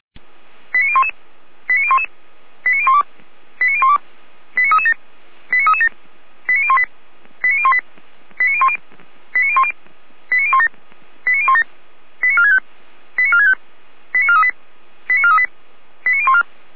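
Five-tone selective call (ZVEI), heard as quick runs of five tones, tones of about 70 ms. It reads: 89011 89011 89012 89012 89337 89337 89018 89018 89010 89010 89017 89017 89465 89465 89438 89438 89014
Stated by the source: ZVEI five-tone call 89319 received by SDR, archived by Feuerwehr Orlamünde (fire brigade).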